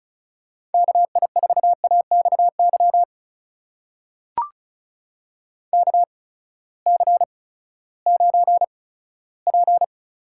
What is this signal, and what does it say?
Morse code, heard as a keyed tone, about 36 wpm, spaced as for 18 wpm, 700 Hz. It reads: KI4AXY KC9P
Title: USA Call Signs - 6 Characters - 8x Character Spacing - 35wpm